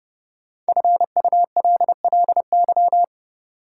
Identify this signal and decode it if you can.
Morse code, heard as a keyed tone, about 31 wpm, 700 Hz. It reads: FULLY
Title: Sentences from Top 1300 Words - New Word Drill Alt - 30wpm